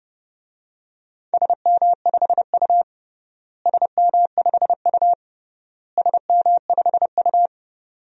Morse code, heard as a keyed tone, about 29 wpm, 700 Hz. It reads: SM5U SM5U SM5U
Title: DXCC-100 Call Signs - 4 Characters - Repeated 3x - 30wpm